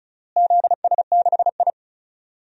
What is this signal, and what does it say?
Morse code, heard as a keyed tone, about 34 wpm, 700 Hz. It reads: ZS6I
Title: DXCC-100 Call Signs - 4 Characters - Repeated 3x - 35wpm